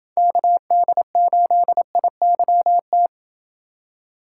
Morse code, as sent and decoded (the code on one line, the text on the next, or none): -.- -.. ---.. .. -.-- -
KD8IYT